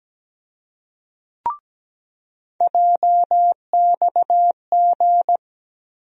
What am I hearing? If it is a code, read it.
Morse code, heard as a keyed tone, about 17 wpm, 700 Hz. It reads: JXG
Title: USA Call Sign Postfixes - 3 Letters 17wpm